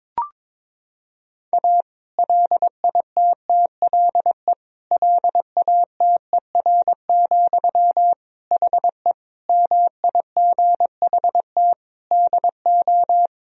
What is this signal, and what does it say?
Morse code, heard as a keyed tone, about 22 wpm, 700 Hz. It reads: A LITTLE LATER, HE MIGHT DO
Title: Sentences from Top 1000 Words - Review Part 4 - 22wpm